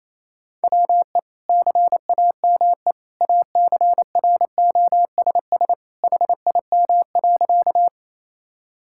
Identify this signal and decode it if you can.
Morse code, heard as a keyed tone, about 28 wpm, 700 Hz. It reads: WE CAME ACROSS HIM.
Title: Sentences from Top 1000 Words - Review Part 5 - 28wpm